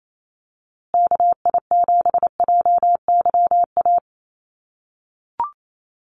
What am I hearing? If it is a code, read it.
Morse code, heard as a keyed tone, about 28 wpm, 700 Hz. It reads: KI7JYA
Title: USA Call Signs Repeated 3x 28wpm